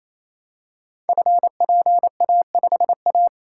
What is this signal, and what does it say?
Morse code, heard as a keyed tone, about 29 wpm, 700 Hz. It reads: FPA5A
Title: Code Groups - 5 Letter:Numbers - 28wpm